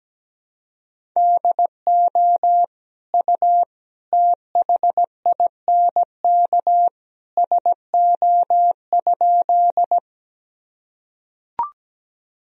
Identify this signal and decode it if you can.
Morse code, heard as a keyed tone, about 17 wpm, 700 Hz. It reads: DO U THINK SO?
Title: Sentences from Top 200 Words and Abbreviations 17wpm